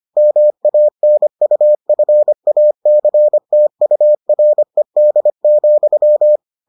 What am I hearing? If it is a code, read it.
Morse code, heard as a keyed tone, about 25 wpm, 600 Hz. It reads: MANUFACTURED,